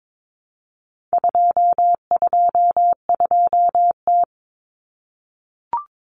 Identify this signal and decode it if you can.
Morse code, heard as a keyed tone, about 22 wpm, 700 Hz. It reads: 222T
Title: Common Cut Numbers - 4-Digit 22wpm